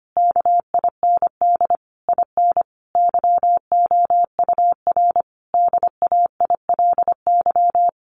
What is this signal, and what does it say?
Morse code, heard as a keyed tone, about 25 wpm, 700 Hz. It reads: KIND IN YOUR DAILY
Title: Sentences from Top 1100 Words 25wpm